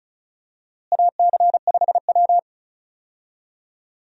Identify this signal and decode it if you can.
Morse code, heard as a keyed tone, about 34 wpm, 700 Hz. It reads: AC5W
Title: USA Call Signs - 4 Characters - Repeated 2x - 35wpm